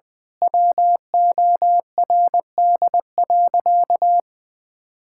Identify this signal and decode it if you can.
Morse code, heard as a keyed tone, about 20 wpm, 700 Hz. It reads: WORD.
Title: Pangram Challenge v2 - 20wpm